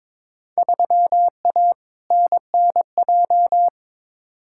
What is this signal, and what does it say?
Morse code, heard as a keyed tone, about 21 wpm, 700 Hz. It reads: 3A NNJ
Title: Field Day Exchanges - 22wpm